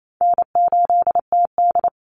Morse code, heard as a keyed tone, about 28 wpm, 700 Hz. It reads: N8TD
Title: Call Signs - 28wpm